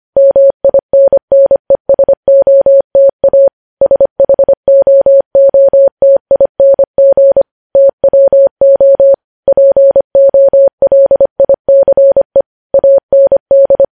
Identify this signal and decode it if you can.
Morse code, heard as a keyed tone, about 25 wpm, 550 Hz. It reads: MINNESOTA SHOOTING TWO POLICE AND
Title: News Headlines at 25